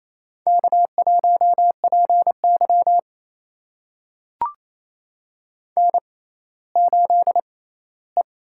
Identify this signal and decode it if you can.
Morse code, heard as a keyed tone, about 28 wpm, spaced as for 16 wpm, 700 Hz. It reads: K1PY N8E
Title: USA Call Signs - 4 Characters - 6x Character Spacing - 28wpm